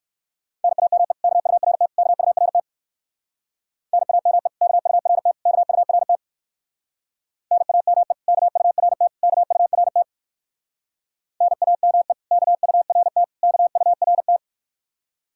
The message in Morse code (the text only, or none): NAME KURT KURT NAME KURT KURT NAME KURT KURT NAME KURT KURT